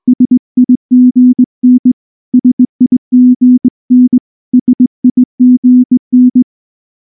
Morse code, heard as a keyed tone, about 19 wpm, 250 Hz. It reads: SIGN SIGN SIGN
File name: morse / morse-training-20WPM-3